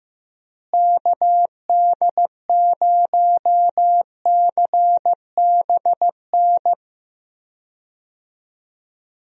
Morse code, heard as a keyed tone, about 14 wpm, 700 Hz. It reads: KD0CBN
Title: USA Call Signs - 6 Characters - Repeated 3x - 15wpm